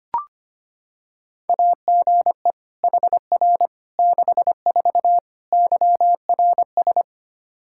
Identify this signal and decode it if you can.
Morse code, heard as a keyed tone, about 25 wpm, 700 Hz. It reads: AGE HR 64 YRS
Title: K1USN SST QSO Practice 25wpm